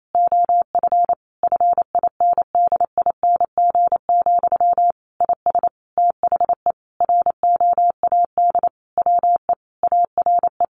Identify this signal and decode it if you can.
Morse code, heard as a keyed tone, about 28 wpm, 700 Hz. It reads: OF FINDING, IS THE ROAD WE ARE
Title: Sentences from Top 1100 Words 28wpm